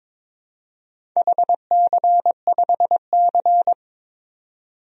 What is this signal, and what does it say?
Morse code, heard as a keyed tone, about 22 wpm, 700 Hz. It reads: HC5C
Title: DXCC-100 Call Signs - 4 Characters - 22wpm